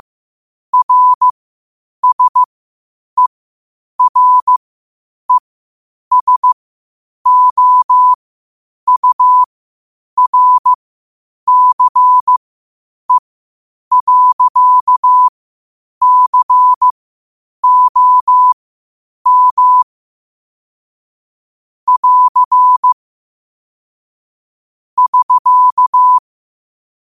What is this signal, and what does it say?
Morse code, heard as a keyed tone, about 15 wpm, spaced as for 8 wpm, 1000 Hz. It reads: RSERESOURCE.COM <AR> <SK>
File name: qotdfiles / 8wpm